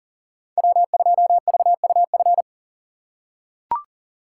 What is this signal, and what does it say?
Morse code, heard as a keyed tone, about 40 wpm, 700 Hz. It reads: W2VUF
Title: K1USN SST QSO Practice - 40wpm